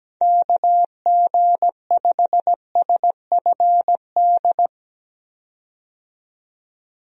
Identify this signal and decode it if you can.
Morse code, heard as a keyed tone, about 17 wpm, 700 Hz. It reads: KG5SFD
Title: USA Call Signs - 6 Characters - 17wpm